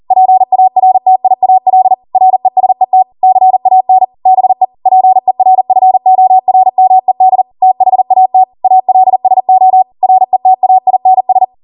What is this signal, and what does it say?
Morse code, heard as a keyed tone, about 40 wpm, 750 Hz. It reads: PARTIAL RESET CAN BE PERFORMED THAT ALSO RETAINS
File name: Morse / 40WPM